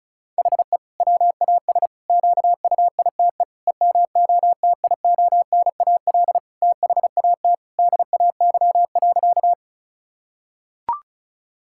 Morse code, heard as a keyed tone, about 34 wpm, 700 Hz. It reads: HE WAS QUITE EMOTIONAL THAT DAY.
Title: Sentences from Top 1000 Words - Review Part 5 - 35wpm